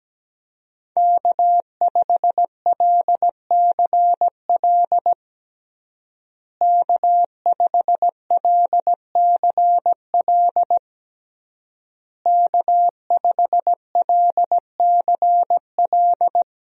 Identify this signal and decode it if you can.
Morse code, heard as a keyed tone, about 17 wpm, 700 Hz. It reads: K5LCL K5LCL K5LCL
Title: USA Call Signs - 5 Characters - Repeated 3x - 17wpm